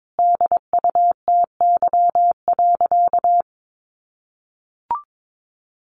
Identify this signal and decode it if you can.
Morse code, heard as a keyed tone, about 22 wpm, 700 Hz. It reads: DUTY.
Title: Sentences from Top 1500 Words 22wpm